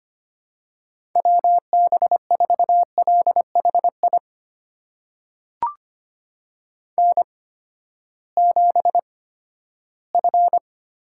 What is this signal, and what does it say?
Morse code, heard as a keyed tone, about 25 wpm, spaced as for 14 wpm, 700 Hz. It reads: WB4LHI N7F
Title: USA Call Signs - 8x Character Spacing - 25wpm